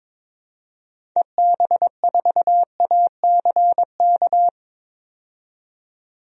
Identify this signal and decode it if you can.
Morse code, heard as a keyed tone, about 22 wpm, 700 Hz. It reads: EB4ACK
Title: DXCC-100 Call Signs - 6 Characters - 22wpm